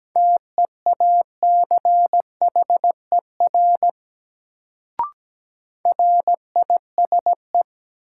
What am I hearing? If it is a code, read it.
Morse code, heard as a keyed tone, about 17 wpm, 700 Hz. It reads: TEACHER RISE
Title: Sentences from Top 900 Words - New Word Drill - 17wpm